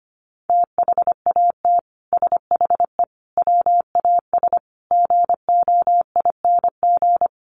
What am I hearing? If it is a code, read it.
Morse code, heard as a keyed tone, about 25 wpm, 700 Hz. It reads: THAT SHE WAS GOING
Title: Sets of 5 Words Encore2 25wpm